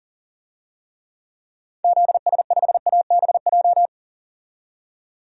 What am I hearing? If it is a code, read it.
Morse code, heard as a keyed tone, about 40 wpm, 700 Hz. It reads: ZS5ABJ
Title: DXCC-100 Call Signs - 6 Characters - 40wpm